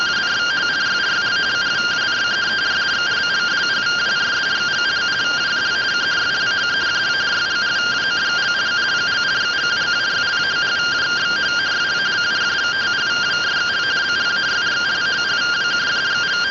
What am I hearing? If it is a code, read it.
Two-tone teleprinter signal, about 170 Hz shift.